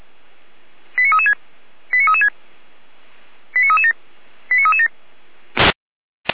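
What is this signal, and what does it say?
Five-tone selective call (ZVEI), heard as quick runs of five tones, tones of about 70 ms. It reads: 89336 89336 89337 89337
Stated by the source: ZVEI five-tone selective call received on SDR.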